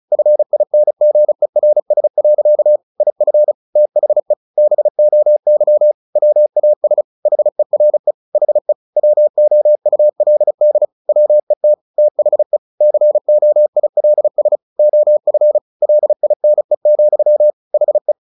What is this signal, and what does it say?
Morse code, heard as a keyed tone, about 34 wpm, 600 Hz. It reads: FINGERS. IF THE BOY WAS HERE HE WOULD WET THE COILS OF LINE, HE
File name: the old man and the sea 2 35wpm